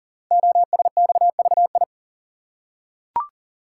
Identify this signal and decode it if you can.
Morse code, heard as a keyed tone, about 41 wpm, 700 Hz. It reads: OSXVI
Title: Code Groups - 5 Letters - 40wpm